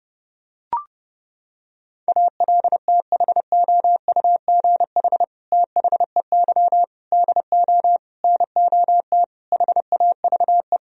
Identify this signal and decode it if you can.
Morse code, heard as a keyed tone, about 30 wpm, 700 Hz. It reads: ALTHOUGH THEY DO NOT HAVE
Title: Sentences from Top 1000 Words - Review Part 3 - 30wpm